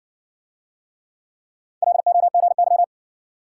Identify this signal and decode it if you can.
Morse code, heard as a keyed tone, about 57 wpm, 700 Hz. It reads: LOGY